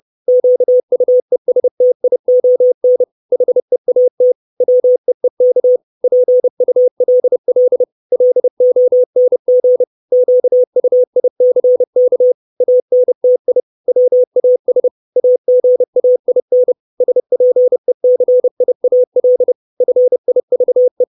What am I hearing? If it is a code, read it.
Morse code, heard as a keyed tone, about 30 wpm, 500 Hz. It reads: QUESTION HEAT WEEK PULL LONG QUICK ANTI WAS AGAIN SPECIAL FIVE
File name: 632 Words 30WPM